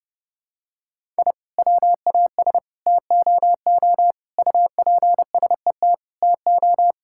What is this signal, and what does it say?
Morse code, heard as a keyed tone, about 32 wpm, 700 Hz. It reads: I WAS TOO UPSET TO
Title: Sentences from Top 1000 Words - Review Part 5 - 30wpm